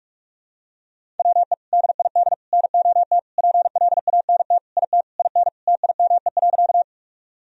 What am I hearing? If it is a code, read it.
Morse code, heard as a keyed tone, about 42 wpm, 700 Hz. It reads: WE DID NOT PLANT IT IN TIME.
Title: Sentences from Top 1000 Words - Review Part 1 - 45wpm